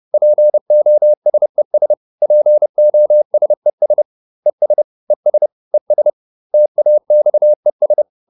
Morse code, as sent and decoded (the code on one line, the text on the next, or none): .--. --- ... ... / .--. --- ... ... / . ... / . ... / . ... / - .- -..- . ...
POSES POSES ES ES ES TAXES